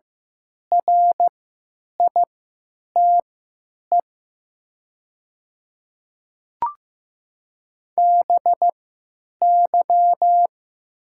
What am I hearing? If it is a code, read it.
Morse code, heard as a keyed tone, about 15 wpm, spaced as for 9 wpm, 700 Hz. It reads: RITE BY
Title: 3x Character Spacing - Top 500 Words - 15wpm